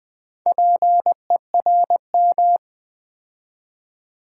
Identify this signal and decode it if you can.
Morse code, heard as a keyed tone, about 20 wpm, 700 Hz. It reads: PERM
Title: Build A Word - 2 Letters At A Time - 6-Letter Words - 20wpm